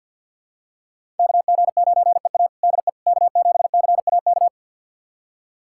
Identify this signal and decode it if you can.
Morse code, heard as a keyed tone, about 52 wpm, 700 Hz. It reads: KG9EA DE K7CAK
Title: USA Callsign DE Callsign - 5 Characters - 50wpm